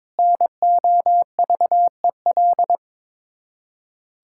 Morse code, heard as a keyed tone, about 22 wpm, 700 Hz. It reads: NOVEL
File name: Difficult Words with Hints - 22wpm